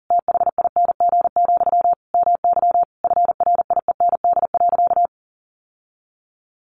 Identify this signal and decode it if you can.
Morse code, heard as a keyed tone, about 37 wpm, 700 Hz. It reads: THING, MY FRIEND.
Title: Sentences from Top 1200 Words 40wpm